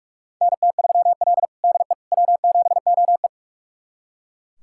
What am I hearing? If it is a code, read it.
Morse code, heard as a keyed tone, about 42 wpm, 700 Hz. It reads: NT3L DE W7OE
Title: USA Callsign DE Callsign - 4 Characters - 45wpm